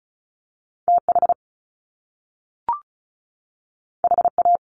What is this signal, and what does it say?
Morse code, heard as a keyed tone, about 36 wpm, 700 Hz. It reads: TH HA
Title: Top Two Letter Combinations in Words 35wpm